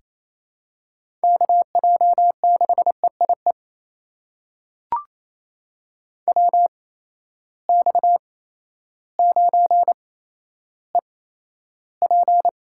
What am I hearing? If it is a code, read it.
Morse code, heard as a keyed tone, about 28 wpm, spaced as for 13 wpm, 700 Hz. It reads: KJ6EIE WX9EP
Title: USA Call Signs - 8x Character Spacing - 28wpm